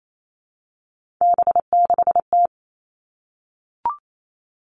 Morse code, heard as a keyed tone, about 28 wpm, 700 Hz. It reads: B6T
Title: Three Challenging Letter-Number Combinations 28wpm